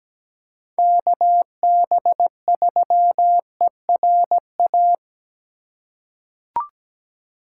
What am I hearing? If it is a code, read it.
Morse code, heard as a keyed tone, about 17 wpm, 700 Hz. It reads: KB3ERA
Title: USA Call Signs - 6 Characters - 8x Character Spacing - 17wpm